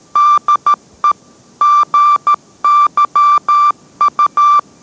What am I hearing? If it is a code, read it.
Morse code, heard as a keyed tone, about 13 wpm, 1180 Hz. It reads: DE GYU